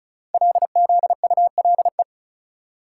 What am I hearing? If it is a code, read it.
Morse code, heard as a keyed tone, about 36 wpm, 700 Hz. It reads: LZULE
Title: Code Groups - 5 Letters - 35wpm